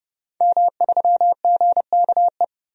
Morse code, heard as a keyed tone, about 32 wpm, 700 Hz. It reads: M3GKE
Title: DXCC-100 Call Signs - 5 Characters - 30wpm